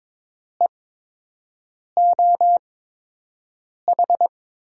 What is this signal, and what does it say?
Morse code, heard as a keyed tone, about 22 wpm, spaced as for 6 wpm, 700 Hz. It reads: EOH
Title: USA Call Signs - 6 Characters - 8x Character Spacing - 22wpm